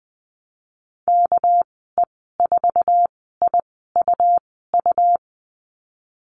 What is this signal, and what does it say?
Morse code, heard as a keyed tone, about 20 wpm, spaced as for 14 wpm, 700 Hz. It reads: KE4IUU
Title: 231 Lesson 37g - Letter x - Call Signs 20wpm